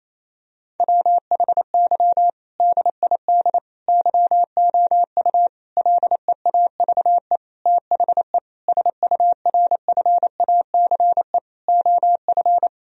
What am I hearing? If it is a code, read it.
Morse code, heard as a keyed tone, about 28 wpm, 700 Hz. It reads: WHY DID YOU LEAVE THE SURFACE OF
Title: Sentences from Top 1000 Words - Review Part 3 - 28wpm